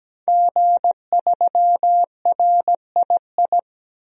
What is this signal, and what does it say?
Morse code, heard as a keyed tone, about 17 wpm, 700 Hz. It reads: G3RII
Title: DXCC-100 Call Signs - 5 Characters - 17wpm